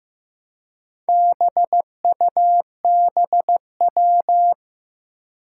Morse code, heard as a keyed tone, about 15 wpm, 700 Hz. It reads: BUBW
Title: Four Challenging Letter-Number Combinations 15wpm